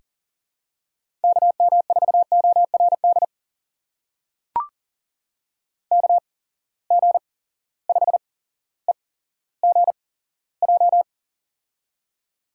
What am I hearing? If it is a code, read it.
Morse code, heard as a keyed tone, about 38 wpm, spaced as for 17 wpm, 700 Hz. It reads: KM4ORD KG5EGJ
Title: USA Call Signs - 8x Character Spacing - 40wpm